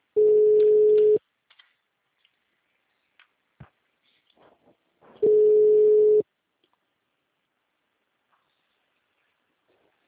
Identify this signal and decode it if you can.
Ringback tone (Europe, 425 Hz).
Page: Americana - SP